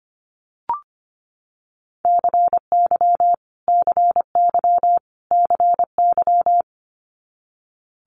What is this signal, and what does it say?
Morse code, heard as a keyed tone, about 25 wpm, 700 Hz. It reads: CY CY CY
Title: Multi-letter Phonemes Repeated 3x 25wpm